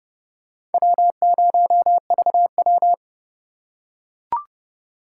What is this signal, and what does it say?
Morse code, heard as a keyed tone, about 32 wpm, 700 Hz. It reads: W0VW
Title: USA Call Signs - 4 Characters - 4x Character Spacing - 30wpm